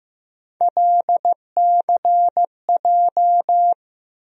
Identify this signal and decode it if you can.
Morse code, heard as a keyed tone, about 15 wpm, 700 Hz. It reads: LCJ
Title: USA Call Sign Postfixes - 3 Letters 15wpm